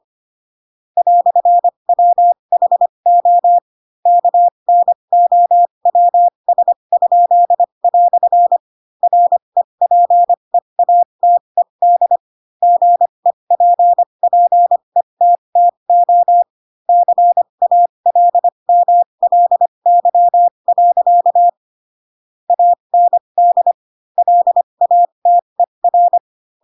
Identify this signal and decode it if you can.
Morse code, heard as a keyed tone, about 25 wpm, spaced as for 22 wpm, 700 Hz. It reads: "WHO KNOWS?" REPEATED GEPPETTO CALMLY. AND LATER
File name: Pinocchio / Pinocchio ch7 18 25 ik1hge